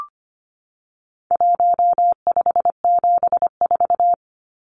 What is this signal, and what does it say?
Morse code, heard as a keyed tone, about 25 wpm, 700 Hz. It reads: 1574
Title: Numbers - 4-Digit 25wpm